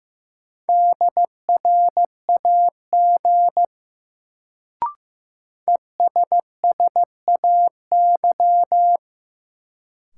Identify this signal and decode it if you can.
Morse code, heard as a keyed tone, about 16 wpm, 700 Hz. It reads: DRAG ESSAY
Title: Sentences from Top 1500 Words - New Word Drill Alt - 15wpm